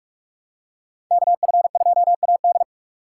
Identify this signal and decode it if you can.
Morse code, heard as a keyed tone, about 42 wpm, 700 Hz. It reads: KF2AD